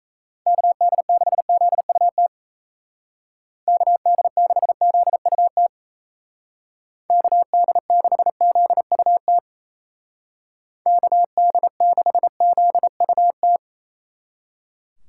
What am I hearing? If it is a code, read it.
Morse code, heard as a keyed tone, about 35 wpm, 700 Hz. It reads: KD6ZUT KD6ZUT KD6ZUT KD6ZUT
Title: Speed Racing - USA 2x3 Call Signs 28wpm